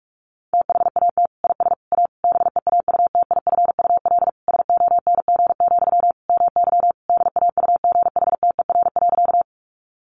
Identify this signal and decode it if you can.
Morse code, heard as a keyed tone, about 44 wpm, 700 Hz. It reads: THAT IS A BEAUTIFUL SONG, MY DAUGHTER.
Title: Sentences from Top 1500 Words 45wpm